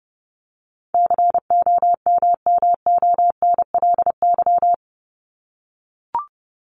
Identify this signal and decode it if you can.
Morse code, heard as a keyed tone, about 30 wpm, 700 Hz. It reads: COMMONLY
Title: LY Suffix - Top 100 Words with Base Words 30wpm